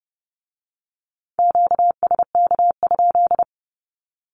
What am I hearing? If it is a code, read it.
Morse code, heard as a keyed tone, about 30 wpm, 700 Hz. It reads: QSK?
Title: Q-Codes with Spoken Meaning - 30wpm